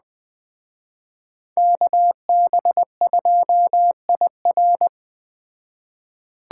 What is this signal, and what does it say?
Morse code, heard as a keyed tone, about 20 wpm, 700 Hz. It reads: KB2IR
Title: Speed Racing - USA Call Signs 17wpm